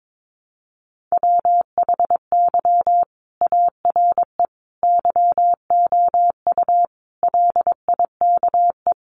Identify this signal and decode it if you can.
Morse code, heard as a keyed tone, about 22 wpm, 700 Hz. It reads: WHY ARE YOU LIKE
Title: Sentences from Top 200 Words 22wpm